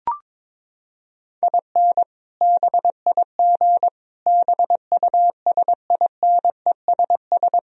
Morse code, heard as a keyed tone, about 22 wpm, 700 Hz. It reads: IN BIG BUSINESS
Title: Sentences from Top 1000 Words - Review Part 1 - 22wpm